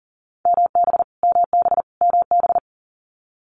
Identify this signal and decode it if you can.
Morse code, heard as a keyed tone, about 39 wpm, 700 Hz. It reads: MB MB MB
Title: Multi-letter Phonemes Repeated 3x 40wpm